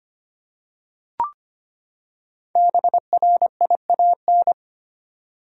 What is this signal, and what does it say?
Morse code, heard as a keyed tone, about 25 wpm, 700 Hz. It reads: BRIAN
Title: USA Common Names - Max 5 Letters - 25wpm